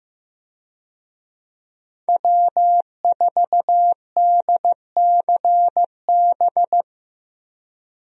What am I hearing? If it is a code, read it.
Morse code, heard as a keyed tone, about 15 wpm, 700 Hz. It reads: W4DCB
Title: USA Call Signs - 5 Characters - Repeated 3x - 15wpm